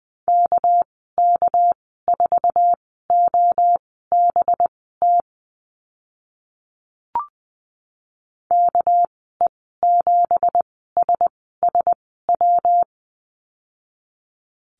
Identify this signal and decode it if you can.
Morse code, heard as a keyed tone, about 20 wpm, spaced as for 14 wpm, 700 Hz. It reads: KK4OBT KE7SSW
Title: 213 Lesson 34g - Number 8 - Call Signs 20wpm